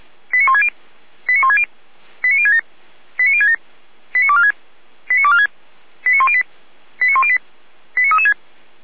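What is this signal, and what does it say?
Five-tone selective call (ZVEI), heard as quick runs of five tones, tones of about 80 ms. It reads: 89160 89160 89067 89067 89246 89246 89108 89108 89336
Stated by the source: ZVEI five-tone selective call 89170 received by SDR.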